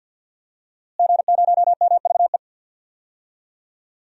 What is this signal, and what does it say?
Morse code, heard as a keyed tone, about 50 wpm, 700 Hz. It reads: G0MVE
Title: DXCC-100 Call Signs - 5 Characters - Repeated 3x - 50wpm